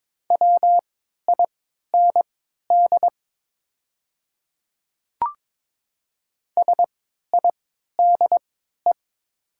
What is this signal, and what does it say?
Morse code, heard as a keyed tone, about 22 wpm, spaced as for 13 wpm, 700 Hz. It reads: WIND SIDE
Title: 3x Character Spacing - Top 500 Words - 22wpm